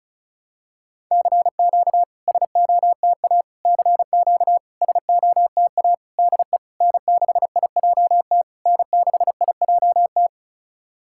Morse code, heard as a keyed tone, about 36 wpm, 700 Hz. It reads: CQ SOTA CQ SOTA DE N6IJT N6IJT
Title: SOTA QSO Practice 35wpm